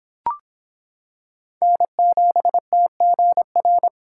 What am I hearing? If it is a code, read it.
Morse code, heard as a keyed tone, about 25 wpm, 700 Hz. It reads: N7TGR